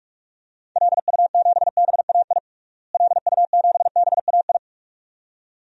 Morse code, heard as a keyed tone, about 46 wpm, 700 Hz. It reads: LU7BAI LU7BAI
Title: DXCC-100 Call Signs - 6 Characters - Repeated 2x - 45wpm